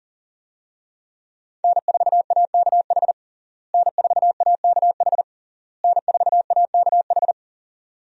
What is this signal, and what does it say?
Morse code, heard as a keyed tone, about 40 wpm, 700 Hz. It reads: N4AKH N4AKH N4AKH
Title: USA Call Signs - 5 Characters - Repeated 3x - 40wpm